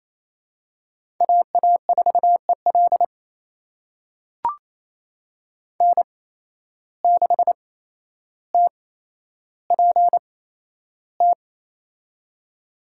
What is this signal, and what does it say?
Morse code, heard as a keyed tone, about 28 wpm, spaced as for 12 wpm, 700 Hz. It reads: AA4EL N6TPT
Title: USA Call Signs - 5 Characters - 8x Character Spacing - 28wpm